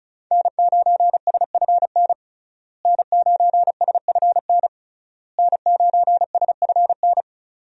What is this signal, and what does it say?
Morse code, heard as a keyed tone, about 36 wpm, 700 Hz. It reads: N9SFN N9SFN N9SFN
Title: USA Call Signs - 5 Characters - Repeated 3x - 35wpm